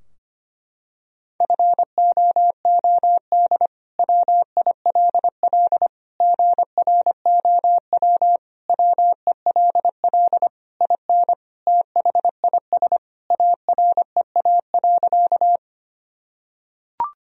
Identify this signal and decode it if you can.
Morse code, heard as a keyed tone, about 25 wpm, 700 Hz. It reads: FOOD WILL GROW WELL IN THIS AREA.
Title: Sentences from Top 1000 Words - Review Part 2 - 25wpm